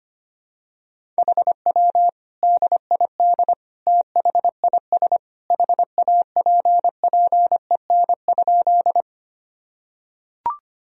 Morse code, heard as a keyed tone, about 25 wpm, 700 Hz. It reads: HW DID THIS HAPPEN?
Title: Sentences from Top 400 Words and Abbreviations 25wpm